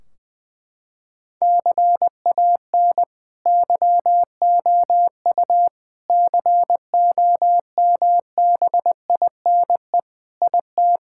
Morse code, heard as a keyed tone, about 20 wpm, 700 Hz. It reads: CAN YOU COMBINE IT
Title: Sentences from Top 1600 Words 20wpm